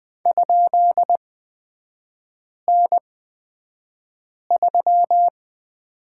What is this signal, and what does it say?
Morse code, heard as a keyed tone, about 20 wpm, spaced as for 5 wpm, 690 Hz. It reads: ?N3
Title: 229 Lesson 37e - Letter x - Three Characters 20wpm